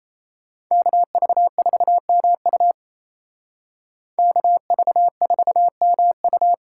Morse code, heard as a keyed tone, about 36 wpm, 700 Hz. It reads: KV4MU KV4MU